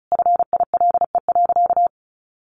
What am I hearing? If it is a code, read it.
Morse code, heard as a keyed tone, about 34 wpm, 700 Hz. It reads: FILE.
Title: Sentences from Top 1200 Words 35wpm